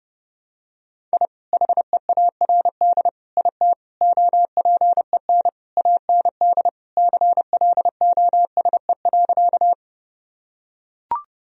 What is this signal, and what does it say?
Morse code, heard as a keyed tone, about 30 wpm, 700 Hz. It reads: I HEARD IT OPEN AND CLOSE.
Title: Sentences from Top 1000 Words - Review Part 1 - 30wpm